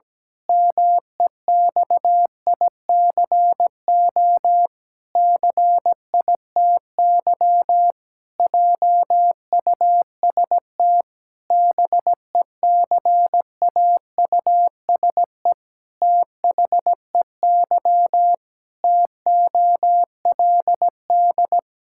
Morse code, heard as a keyed tone, about 17 wpm, 700 Hz. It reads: MEXICO CITY JUST BECAUSE THEY TOLD
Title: Pangram Challenge v2 - 17wpm